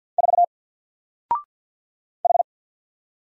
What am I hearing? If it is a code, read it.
Morse code, heard as a keyed tone, about 48 wpm, 700 Hz. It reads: 4 H
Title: Single Letter-Number Alt 50wpm